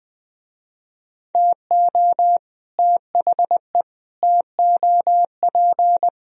Morse code, heard as a keyed tone, about 24 wpm, 700 Hz. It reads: TO THE TOP
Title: Speed-Racing - Sets of 5 Words 20wpm